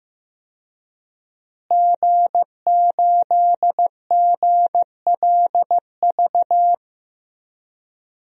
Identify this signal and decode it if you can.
Morse code, heard as a keyed tone, about 15 wpm, 700 Hz. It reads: G8GLV
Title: DXCC-100 Call Signs - 5 Characters - Repeated 3x - 15wpm